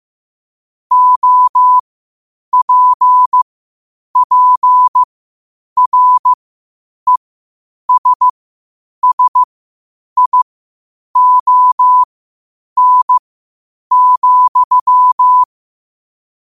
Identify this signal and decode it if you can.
Morse code, heard as a keyed tone, about 15 wpm, spaced as for 8 wpm, 1000 Hz. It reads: OPPRESSION,
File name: qotdfiles / 8wpm